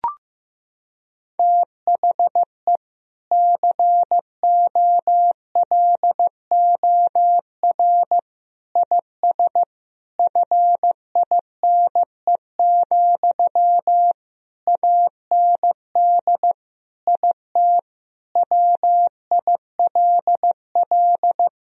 Morse code, heard as a keyed tone, about 15 wpm, 700 Hz. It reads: THE COLOR IS FINE, AND IT WILL
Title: Sentences from Top 1000 Words - Review Part 2 - 15wpm